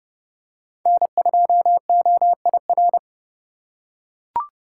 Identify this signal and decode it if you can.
Morse code, heard as a keyed tone, about 30 wpm, 700 Hz. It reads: N2OIR
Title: USA Call Signs - 3x Character Spacing - 30wpm